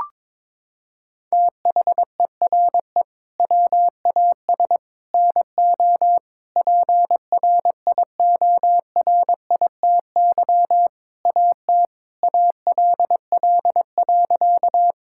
Morse code, heard as a keyed tone, about 22 wpm, 700 Hz. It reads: THERE WAS NO PRIORITY AT ALL.
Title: Sentences from Top 1600 Words 22wpm